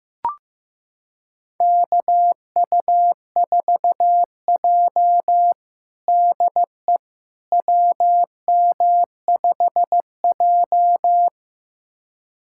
Morse code, heard as a keyed tone, about 15 wpm, 700 Hz. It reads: KU4J DE WM5J
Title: USA Callsign DE Callsign - 4 Characters - 15wpm